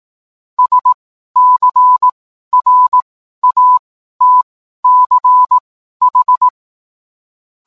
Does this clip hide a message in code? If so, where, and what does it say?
Yes: Morse code from 0.6 to 6.5 s, heard as a keyed tone, about 18 wpm, spaced as for 12 wpm, 1000 Hz. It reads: SCRATCH